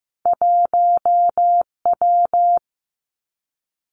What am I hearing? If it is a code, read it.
Morse code, heard as a keyed tone, about 15 wpm, 700 Hz. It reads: 1W
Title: Two Challenging Letter-Number Combinations 15wpm